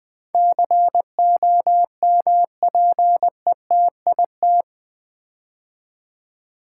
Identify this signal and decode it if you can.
Morse code, heard as a keyed tone, about 20 wpm, 700 Hz. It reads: COMPETIT
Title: Build A Word - 4 Letters At A Time - 11-Letter Words - 20wpm